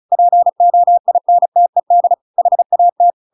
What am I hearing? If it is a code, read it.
Morse code, heard as a keyed tone, about 34 wpm, 700 Hz. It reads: POINTED HAT
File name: Pinocchio / Pinocchio ch8 35 35 ik1hge